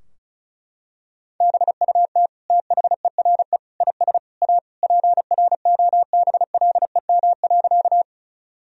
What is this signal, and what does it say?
Morse code, heard as a keyed tone, about 34 wpm, 700 Hz. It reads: BUT THERE IS A PROBLEM.
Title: Sentences from Top 1000 Words - Review Part 2 - 35wpm